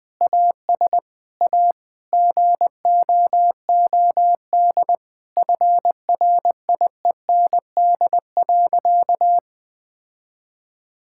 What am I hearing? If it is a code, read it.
Morse code, heard as a keyed tone, about 20 wpm, 700 Hz. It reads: AS A GOOD FRIEND.
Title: Sentences with 4-Word Phrases - 20wpm